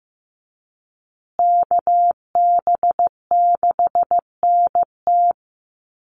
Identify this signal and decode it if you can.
Morse code, heard as a keyed tone, about 15 wpm, 700 Hz. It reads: KB6NT
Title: USA Call Signs Repeated 3x 15wpm